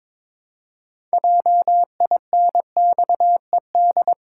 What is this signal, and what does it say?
Morse code, heard as a keyed tone, about 22 wpm, 700 Hz. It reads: JINXED
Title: Pangram Challenge v2 - 22wpm